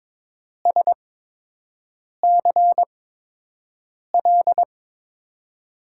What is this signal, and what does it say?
Morse code, heard as a keyed tone, about 22 wpm, spaced as for 6 wpm, 700 Hz. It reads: SCL
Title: USA Call Signs - 6 Characters - 8x Character Spacing - 22wpm